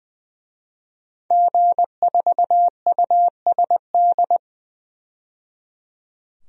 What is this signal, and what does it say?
Morse code, heard as a keyed tone, about 20 wpm, 700 Hz. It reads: G4USD
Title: DXCC-100 Call Signs - 5 Characters - 20wpm